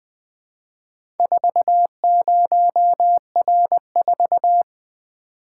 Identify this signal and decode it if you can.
Morse code, heard as a keyed tone, about 20 wpm, 700 Hz. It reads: 40R4
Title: Four Letter-Number Combinations - 20wpm